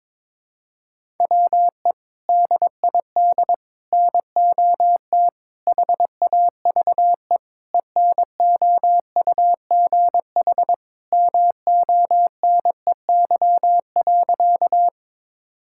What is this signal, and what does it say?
Morse code, heard as a keyed tone, about 22 wpm, 700 Hz. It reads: WE DID NOT HAVE ENOUGH MONEY.
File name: practice / 22wpm / Sentences with 4-Word Phrases - 22wpm